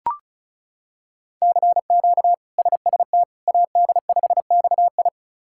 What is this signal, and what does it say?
Morse code, heard as a keyed tone, about 36 wpm, 700 Hz. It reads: CQ SST AD5XI
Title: K1USN SST QSO Practice - 35wpm